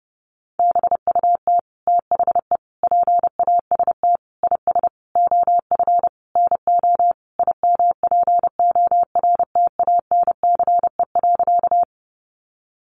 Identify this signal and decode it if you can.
Morse code, heard as a keyed tone, about 30 wpm, 700 Hz. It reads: BUT THE PAST IS OF NO IMPORTANCE.
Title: Sentences from Top 1200 Words 30wpm